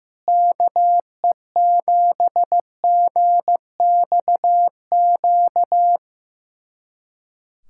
Morse code, heard as a keyed tone, about 15 wpm, 700 Hz. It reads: KE7GXQ
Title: USA Callsign DE Callsign 15wpm